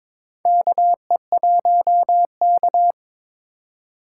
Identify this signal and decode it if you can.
Morse code, heard as a keyed tone, about 22 wpm, 700 Hz. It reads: KE1K
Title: USA Call Signs - 4 Characters - 2x Character Spacing - 22wpm